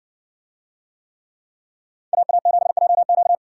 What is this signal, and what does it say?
Morse code, heard as a keyed tone, about 57 wpm, 700 Hz. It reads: AA7JQ